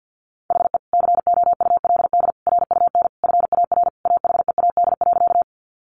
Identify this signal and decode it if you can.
Morse code, heard as a keyed tone, about 50 wpm, 700 Hz. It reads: HE COULD RUN FAR AHEAD.
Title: Sentences from Top 1100 Words 50wpm